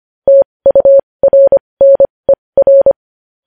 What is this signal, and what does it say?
Morse code, heard as a keyed tone, about 25 wpm, spaced as for 20 wpm, 550 Hz. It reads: TURNER
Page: News headlines in Morse code at 20 wpm